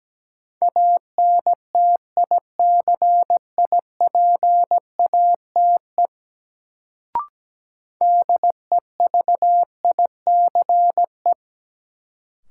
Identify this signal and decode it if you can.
Morse code, heard as a keyed tone, about 17 wpm, 700 Hz. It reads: ANTICIPATE DEVICE
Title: Sentences from Top 1500 Words - New Word Drill Alt - 17wpm